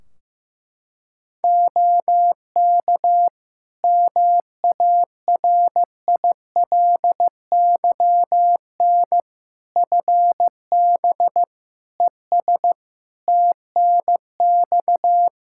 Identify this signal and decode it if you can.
Morse code, heard as a keyed tone, about 16 wpm, 700 Hz. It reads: OK MARILYN FB ES TNX
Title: Practice QSO - Exchange 2 - Info - 15wpm